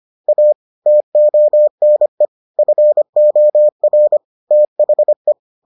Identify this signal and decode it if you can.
Morse code, heard as a keyed tone, about 25 wpm, 600 Hz. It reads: A TONE FOR THE